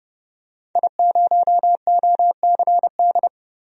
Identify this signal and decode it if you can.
Morse code, heard as a keyed tone, about 30 wpm, 700 Hz. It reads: I0OCD
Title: DXCC-100 Call Signs - 5 Characters - Repeated 2x - 30wpm